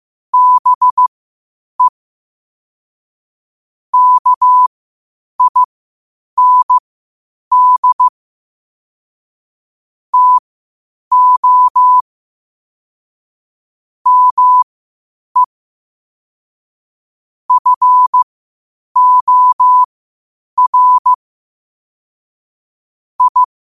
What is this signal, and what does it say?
Morse code, heard as a keyed tone, about 15 wpm, spaced as for 8 wpm, 1000 Hz. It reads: BE KIND TO ME FOR I
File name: qotdfiles / 8wpm